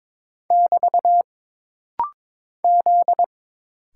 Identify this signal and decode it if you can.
Morse code, heard as a keyed tone, about 22 wpm, 700 Hz. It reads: <BT> Z
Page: This rapid-fire timing is geared to push your ICR (Instant Character Recognition) skills.